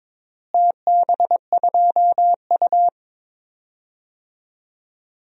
Morse code, heard as a keyed tone, about 22 wpm, 700 Hz. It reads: TB2U
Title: DXCC-100 Call Signs - 4 Characters - Repeated 3x - 22wpm